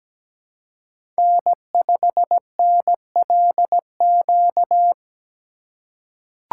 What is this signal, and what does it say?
Morse code, heard as a keyed tone, about 17 wpm, 700 Hz. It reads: N5NLQ
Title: USA Call Signs - 5 Characters - 2x Character Spacing - 17wpm